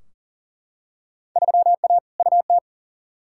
Morse code, heard as a keyed tone, about 40 wpm, 700 Hz. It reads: 3A UT
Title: Field Day Exchanges - 40wpm